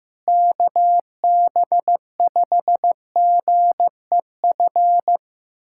Morse code, heard as a keyed tone, about 15 wpm, 700 Hz. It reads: KB5GEF
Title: USA Callsign DE Callsign - 6 Characters - 15wpm